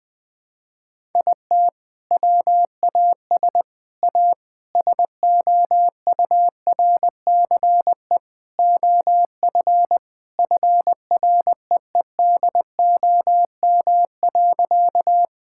Morse code, heard as a keyed tone, about 20 wpm, 700 Hz. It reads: IT WAS A SOURCE OF FREEDOM.
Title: Sentences from Top 1000 Words - Review Part 5 - 20wpm